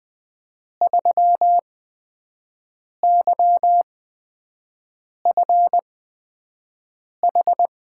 Morse code, heard as a keyed tone, about 20 wpm, spaced as for 5 wpm, 700 Hz. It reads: 3YFH
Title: USA Call Signs - 5 Characters - 8x Character Spacing - 20wpm